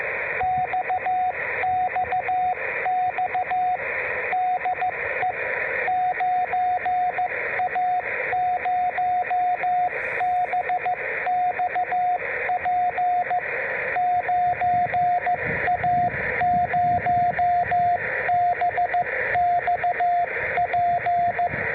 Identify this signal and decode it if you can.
Morse code, heard as a keyed tone, about 14 wpm, 690 Hz, with radio noise. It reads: XXX DE 9A0BXP 9A0BXP